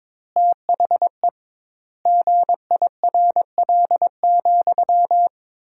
Morse code, heard as a keyed tone, about 22 wpm, 700 Hz. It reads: THE GIRL,
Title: 2x Word-Spacing - Sentences from Top 400 Words - 22wpm